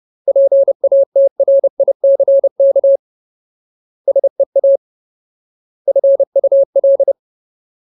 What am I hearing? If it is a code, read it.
Morse code, heard as a keyed tone, about 30 wpm, 550 Hz. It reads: PATRICK SEA FUL
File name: Ultimate Word List / 30WPM Ultimate